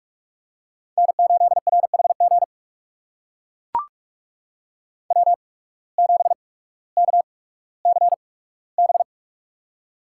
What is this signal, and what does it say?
Morse code, heard as a keyed tone, about 44 wpm, spaced as for 19 wpm, 700 Hz. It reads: N8RHG W7KCB
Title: USA Call Signs - 8x Character Spacing - 45wpm